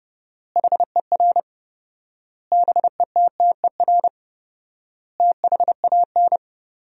Morse code, heard as a keyed tone, about 30 wpm, 700 Hz. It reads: HER BETTER THAN
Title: 4x Word-Spacing - Sentences from Top 400 Words - 30wpm